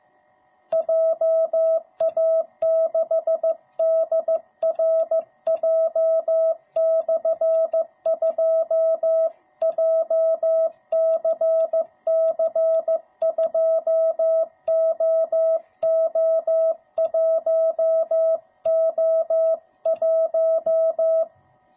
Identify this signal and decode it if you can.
Morse code, heard as a keyed tone, about 15 wpm, 660 Hz. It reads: JA6DRJ/2 JCC2OO1O1